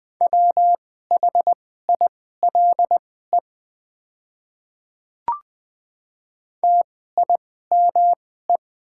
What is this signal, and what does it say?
Morse code, heard as a keyed tone, about 20 wpm, spaced as for 15 wpm, 700 Hz. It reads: WHILE TIME
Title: 2x Character Spacing - Top 300 Words - 20wpm